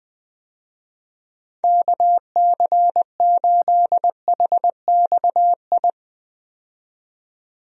Morse code, heard as a keyed tone, about 20 wpm, 700 Hz. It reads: KC8HXI
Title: USA Call Signs - 6 Characters - 20wpm